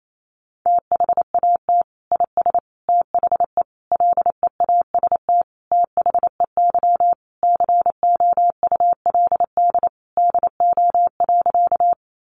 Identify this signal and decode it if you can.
Morse code, heard as a keyed tone, about 28 wpm, 700 Hz. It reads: THAT IS THE LEAST THEY COULD DO.
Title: Sentences from Top 700 Words 28wpm